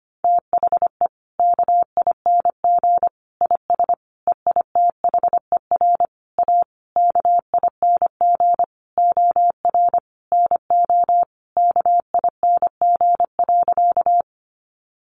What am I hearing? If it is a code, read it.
Morse code, heard as a keyed tone, about 25 wpm, 700 Hz. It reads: THE KING IS EITHER A KING OR NO KING.
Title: Sentences from Top 600 Words 25wpm